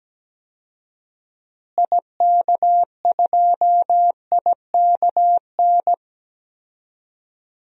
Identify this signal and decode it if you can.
Morse code, heard as a keyed tone, about 17 wpm, 700 Hz. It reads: IK2IKN
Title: DXCC-100 Call Signs - 6 Characters - Repeated 3x - 17wpm